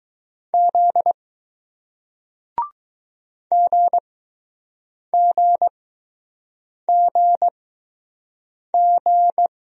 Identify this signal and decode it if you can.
Morse code, heard as a keyed tone, about 20 wpm, 700 Hz. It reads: Z G G G G